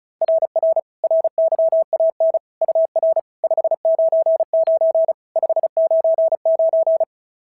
Morse code, heard as a keyed tone, about 33 wpm, 650 Hz. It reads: RR RYAN UR 599 599
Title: SOTA QSO Practice 35wpm